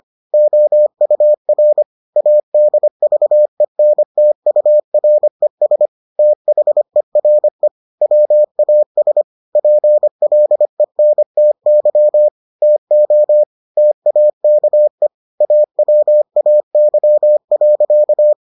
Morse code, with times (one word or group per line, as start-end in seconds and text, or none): 0.3-1.8 OUR
2.2-5.9 ADVENTURES
6.2-7.7 THERE
8.0-9.2 WAS
9.6-12.3 PLENTY
12.6-13.4 TO
13.8-15.1 TAKE
15.4-18.3 AWAY.